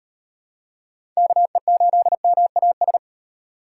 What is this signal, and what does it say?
Morse code, heard as a keyed tone, about 38 wpm, 700 Hz. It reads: KE8MAS